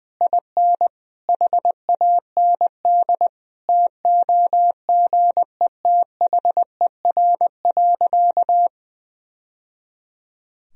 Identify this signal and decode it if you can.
Morse code, heard as a keyed tone, about 21 wpm, 700 Hz. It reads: IN HAND TOGETHER.
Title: Sentences from Top 1000 Words - Review Part 4 - 20wpm